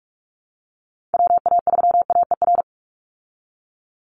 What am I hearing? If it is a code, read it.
Morse code, heard as a keyed tone, about 44 wpm, 700 Hz. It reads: WA3AER
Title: Call Signs Repeated 2x - 45wpm